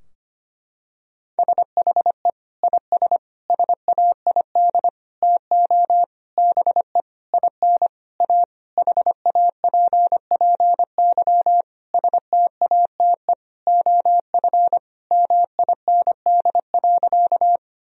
Morse code, heard as a keyed tone, about 25 wpm, 700 Hz. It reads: SHE IS SAID TO BE IN A HAPPY STATE OF MIND.
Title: Sentences from Top 1000 Words - Review Part 2 - 25wpm